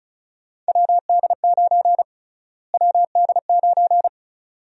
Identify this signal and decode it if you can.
Morse code, heard as a keyed tone, about 36 wpm, 700 Hz. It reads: WD9 WD9
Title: USA Call Sign Prefixes - Repeated 2x - 35wpm